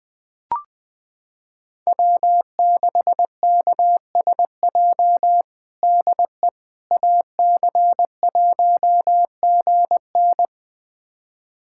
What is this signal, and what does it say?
Morse code, heard as a keyed tone, about 21 wpm, 700 Hz. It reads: W6KSJ DE AC1GN
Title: USA Callsign DE Callsign - 5 Characters - 20wpm